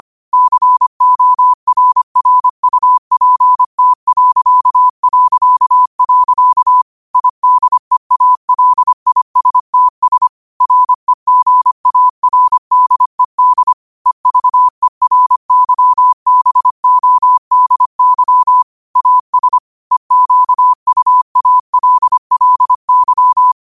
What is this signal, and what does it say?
Morse code, heard as a keyed tone, about 25 wpm, 1000 Hz. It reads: CORRUPT... IDEALISTS REGARDED EVERYBODY AS EQUALLY
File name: qotdfiles / 25wpm